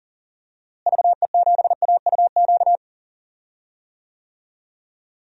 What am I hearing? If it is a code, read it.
Morse code, heard as a keyed tone, about 37 wpm, 700 Hz. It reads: VE7AUQ
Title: DXCC-100 Call Signs - 6 Characters - Repeated 3x - 40wpm